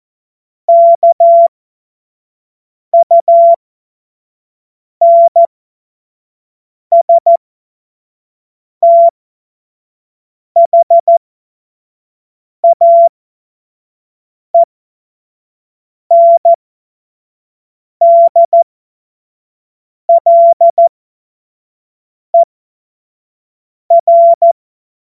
Morse code, heard as a keyed tone, about 14 wpm, spaced as for 5 wpm, 680 Hz. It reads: KUNSTHAENDLER
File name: Morsekurs / Witze / Witz 20. Januar 2026 25 F70